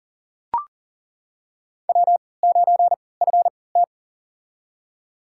Morse code, heard as a keyed tone, about 40 wpm, spaced as for 23 wpm, 700 Hz. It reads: W9FT
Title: USA Call Signs - 4 Characters - 3x Character Spacing - 40wpm